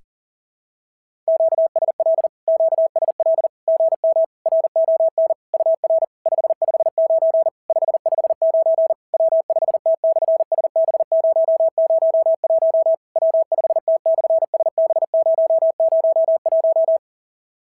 Morse code, heard as a keyed tone, about 40 wpm, 660 Hz. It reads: QSL QSL GM RON UR 559 559 W5T/SB001 W5T/SB001
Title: SOTA QSO Practice 40wpm